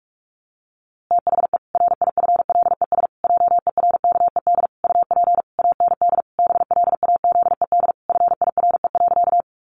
Morse code, heard as a keyed tone, about 44 wpm, 700 Hz. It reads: THE RIFLES JERKED UP AND BLAZED FIRE.
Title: A Sound Of Thunder 45wpm